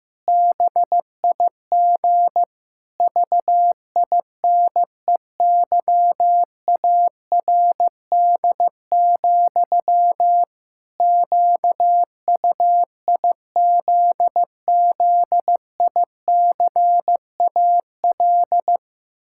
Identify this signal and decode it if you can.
Morse code, heard as a keyed tone, about 15 wpm, 700 Hz. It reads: BIG VINEYARD, QUIZZICAL
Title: Pangram Challenge v2 - 15wpm